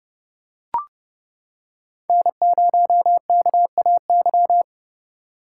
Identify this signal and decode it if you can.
Morse code, heard as a keyed tone, about 29 wpm, 700 Hz. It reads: N0KAY